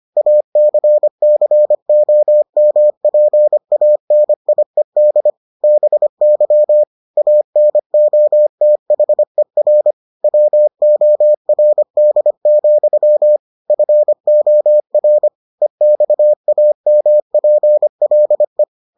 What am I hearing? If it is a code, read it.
Morse code, heard as a keyed tone, about 25 wpm, 600 Hz. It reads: ACCOMPANIED BY ANOTHER WORD, FOR EXAMPLE